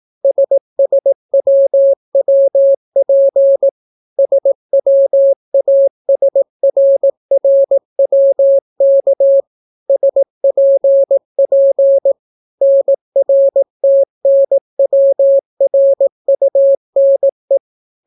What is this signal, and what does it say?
Morse code, heard as a keyed tone, about 18 wpm, 550 Hz. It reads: SSWWP SWASRRWK SPP NRTNWRUNE